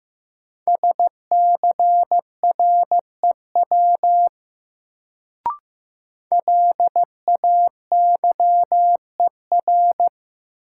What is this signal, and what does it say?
Morse code, heard as a keyed tone, about 15 wpm, 700 Hz. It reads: SCREW LAYER
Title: Sentences from Top 1600 Words - New Word Drill - 15wpm